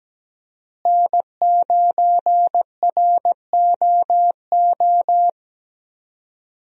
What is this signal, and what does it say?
Morse code, heard as a keyed tone, about 17 wpm, 700 Hz. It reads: N9ROO
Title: USA Call Signs - 5 Characters - Repeated 3x - 17wpm